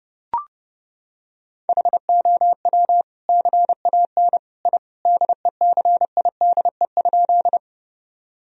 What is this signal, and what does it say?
Morse code, heard as a keyed tone, about 30 wpm, 700 Hz. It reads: HOW CAN I DECIDE?
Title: Sentences from Top 1000 Words - Review Part 3 - 30wpm